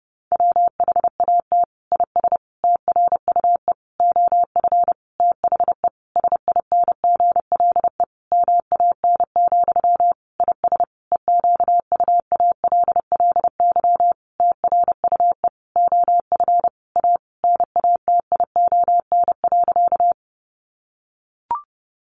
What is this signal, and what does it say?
Morse code, heard as a keyed tone, about 30 wpm, 700 Hz. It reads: WHAT IS TRUE OF THE SINGLE MAN, IS EQUALLY TRUE OF A NATION.
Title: Sentences from Top 1400 Words 30wpm